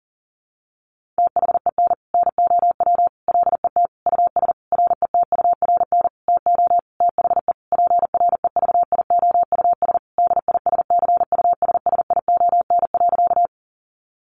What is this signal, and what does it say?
Morse code, heard as a keyed tone, about 40 wpm, 700 Hz. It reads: THEN NOW LET US RETURN TO THE PREVIOUS DISCUSSION.
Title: Sentences from Top 1500 Words 40wpm